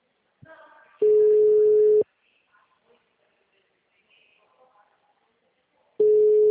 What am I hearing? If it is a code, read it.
Ringback tone (Europe, 425 Hz).